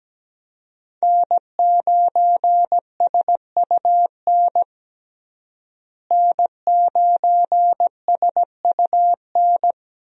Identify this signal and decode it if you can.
Morse code, heard as a keyed tone, about 17 wpm, 700 Hz. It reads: N9SUN N9SUN
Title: USA Call Signs - 5 Characters - Repeated 3x - 17wpm